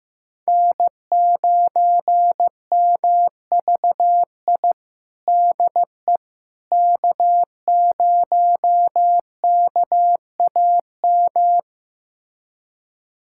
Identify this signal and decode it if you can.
Morse code, heard as a keyed tone, about 15 wpm, 700 Hz. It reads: N9MVI DE K0KAM
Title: USA Callsign DE Callsign - 5 Characters - 15wpm